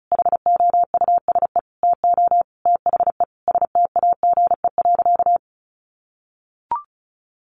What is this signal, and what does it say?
Morse code, heard as a keyed tone, about 34 wpm, 700 Hz. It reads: HOUSE TO THE STAGE.
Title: Sentences from Top 1200 Words 35wpm